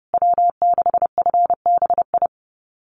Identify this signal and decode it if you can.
Morse code, heard as a keyed tone, about 30 wpm, 700 Hz. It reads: W6FBI
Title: USA Call Signs Repeated 2x 30wpm